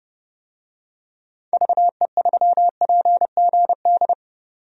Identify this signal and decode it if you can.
Morse code, heard as a keyed tone, about 30 wpm, 700 Hz. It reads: VE3PGD
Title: DXCC-100 Call Signs - 6 Characters - 30wpm